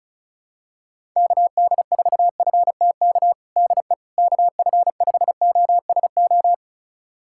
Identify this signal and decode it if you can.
Morse code, heard as a keyed tone, about 36 wpm, 700 Hz. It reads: KD4FTK DE KF5OSO
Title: USA Callsign DE Callsign 35wpm